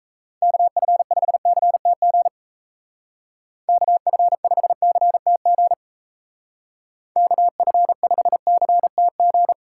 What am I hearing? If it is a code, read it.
Morse code, heard as a keyed tone, about 38 wpm, 700 Hz. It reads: KF5CTG KF5CTG KF5CTG